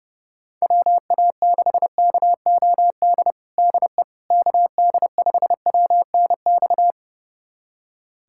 Morse code, heard as a keyed tone, about 30 wpm, 700 Hz. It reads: WA6KOD DE KD5WNX
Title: USA Callsign DE Callsign - 6 Characters - 30wpm